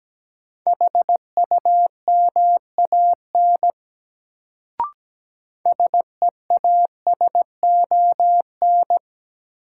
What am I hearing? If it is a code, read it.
Morse code, heard as a keyed tone, about 17 wpm, 700 Hz. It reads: HUMAN SEASON
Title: Sets of 5 Words - New Word Drill - 17wpm